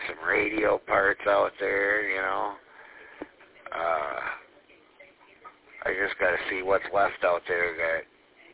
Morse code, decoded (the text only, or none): none